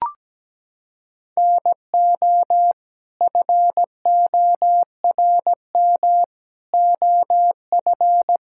Morse code, heard as a keyed tone, about 17 wpm, 700 Hz. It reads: NO FORM OF
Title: Sentences from Top 1000 Words - Review Part 6 - 17wpm